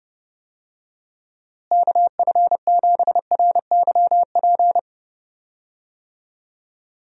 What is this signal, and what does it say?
Morse code, heard as a keyed tone, about 32 wpm, 700 Hz. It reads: KF7RYP
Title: USA Call Signs - 6 Characters - Repeated 3x - 30wpm